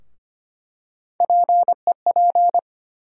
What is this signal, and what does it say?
Morse code, heard as a keyed tone, about 25 wpm, 700 Hz. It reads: PEP
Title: USA Call Sign Postfixes - 3 Letters 25wpm